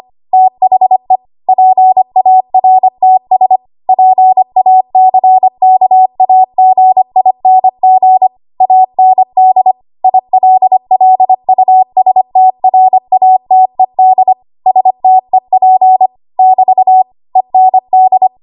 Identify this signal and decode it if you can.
Morse code, heard as a keyed tone, about 25 wpm, 750 Hz. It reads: THE PARTS PACKAGING AND ILLUSTRATED STEP <BT> END
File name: Morse / 25WPM